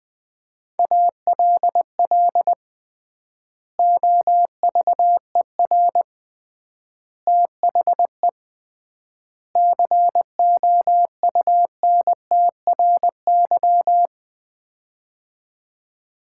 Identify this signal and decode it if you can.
Morse code, heard as a keyed tone, about 20 wpm, 700 Hz. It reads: ALL OVER THE COUNTRY
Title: 3x Word-Spacing - Sets of 4 Words - 20wpm